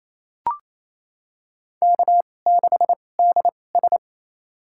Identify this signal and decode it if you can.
Morse code, heard as a keyed tone, about 27 wpm, spaced as for 20 wpm, 700 Hz. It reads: K6DS
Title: USA Call Signs - 2x Character Spacing - 28wpm